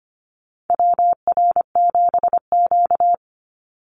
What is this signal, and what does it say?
Morse code, heard as a keyed tone, about 25 wpm, 700 Hz. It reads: WR7Q
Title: Call Signs - 25wpm